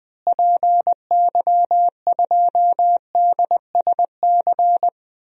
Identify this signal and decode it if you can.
Morse code, heard as a keyed tone, about 21 wpm, 700 Hz. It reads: PY2DSC
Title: DXCC-100 Call Signs - 6 Characters - Repeated 2x - 20wpm